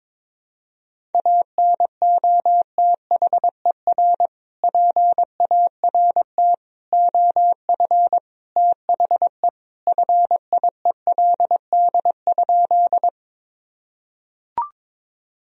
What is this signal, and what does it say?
Morse code, heard as a keyed tone, about 22 wpm, 700 Hz. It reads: ANOTHER PART OF THE FIELD?
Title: Sentences from Top 1000 Words - Review Part 2 - 22wpm